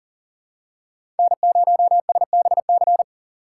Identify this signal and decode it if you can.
Morse code, heard as a keyed tone, about 40 wpm, 700 Hz. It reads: N0SBC
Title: USA Call Signs - 5 Characters - 3x Character Spacing - 40wpm